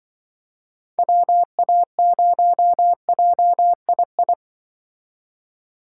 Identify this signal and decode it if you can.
Morse code, heard as a keyed tone, about 24 wpm, 700 Hz. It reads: WA0JII
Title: Speed Racing - USA Call Signs 20wpm